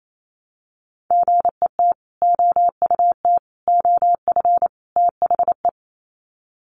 Morse code, heard as a keyed tone, about 28 wpm, 700 Hz. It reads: GET OUT OF THE
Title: Sets of 4 Words 28wpm